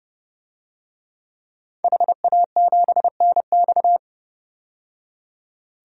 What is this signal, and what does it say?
Morse code, heard as a keyed tone, about 30 wpm, 700 Hz. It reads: HA7NX
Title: DXCC-100 Call Signs - 5 Characters - 30wpm